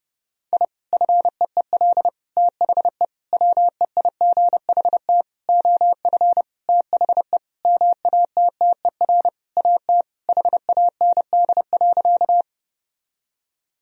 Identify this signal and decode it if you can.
Morse code, heard as a keyed tone, about 30 wpm, 700 Hz. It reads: I FEEL THE WEIGHT OF THE MATTER AT HAND.
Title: Sentences from Top 1000 Words - Review Part 3 - 30wpm